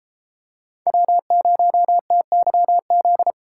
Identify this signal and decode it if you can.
Morse code, heard as a keyed tone, about 33 wpm, 700 Hz. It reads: W0TYZ